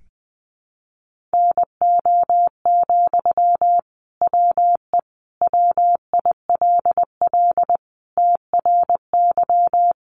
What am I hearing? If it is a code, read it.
Morse code, heard as a keyed tone, about 20 wpm, 700 Hz. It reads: NO, WE WILL TRY
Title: Sentences from Top 1100 Words 20wpm